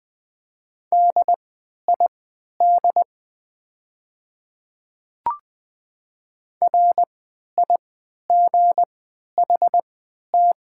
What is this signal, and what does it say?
Morse code, heard as a keyed tone, about 21 wpm, spaced as for 11 wpm, 700 Hz. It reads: DID RIGHT
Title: 3x Character Spacing - Top 200 Words - 20wpm